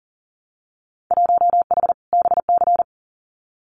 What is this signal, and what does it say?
Morse code, heard as a keyed tone, about 39 wpm, 700 Hz. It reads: 1H BC
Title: Winter Field Day Exchanges 40wpm